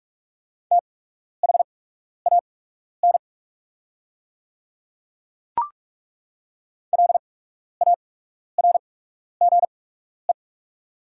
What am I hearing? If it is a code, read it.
Morse code, heard as a keyed tone, about 47 wpm, spaced as for 12 wpm, 690 Hz. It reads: THAN LARGE
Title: 8x Character Spacing - Top 300 Words - 45wpm